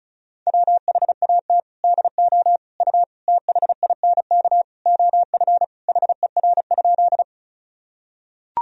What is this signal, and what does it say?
Morse code, heard as a keyed tone, about 34 wpm, 700 Hz. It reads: WHAT DO U THINK OF HER?
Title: Sentences from Top 200 Words and Abbreviations 35wpm